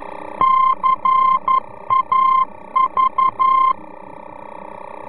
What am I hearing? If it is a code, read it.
Morse code, heard as a keyed tone, about 11 wpm, 1050 Hz. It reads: CAV
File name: france / mx / CAV-VOR-F